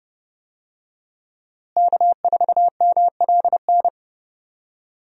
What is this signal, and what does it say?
Morse code, heard as a keyed tone, about 31 wpm, 700 Hz. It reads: K4MLN
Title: USA Call Signs - 5 Characters - Repeated 3x - 30wpm